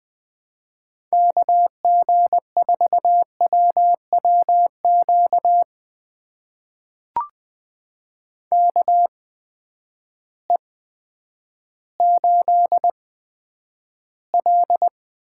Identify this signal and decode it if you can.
Morse code, heard as a keyed tone, about 20 wpm, spaced as for 10 wpm, 700 Hz. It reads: KG4WWQ KE8L
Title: USA Call Signs - 6 Characters - 8x Character Spacing - 20wpm